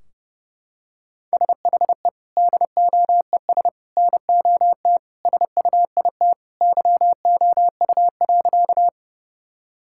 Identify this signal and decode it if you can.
Morse code, heard as a keyed tone, about 29 wpm, 700 Hz. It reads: SHE DOES NOT SUIT YOU.
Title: Sentences from Top 1600 Words 30wpm